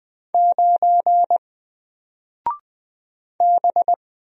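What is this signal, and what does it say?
Morse code, heard as a keyed tone, about 20 wpm, 700 Hz. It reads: 9 B
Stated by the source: Single letter and number Morse drill at 20 wpm, from N4LCD.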